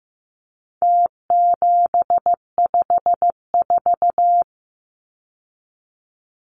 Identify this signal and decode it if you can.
Morse code, heard as a keyed tone, about 15 wpm, 700 Hz. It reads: T754
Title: Numbers - Contest Style - 4-Digit 15wpm